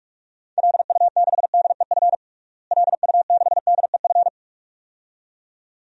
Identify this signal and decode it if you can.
Morse code, heard as a keyed tone, about 46 wpm, 700 Hz. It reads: LU6DEF LU6DEF
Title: DXCC-100 Call Signs - 6 Characters - Repeated 2x - 45wpm